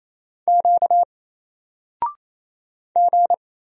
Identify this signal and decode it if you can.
Morse code, heard as a keyed tone, about 27 wpm, 700 Hz. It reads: Q G
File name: Single-Letter-Number-28wpm